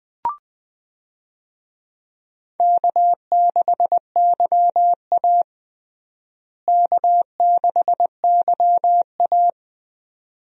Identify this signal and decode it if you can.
Morse code, heard as a keyed tone, about 20 wpm, 700 Hz. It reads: K6YA K6YA
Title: USA Call Signs - 4 Characters - Repeated 2x - 20wpm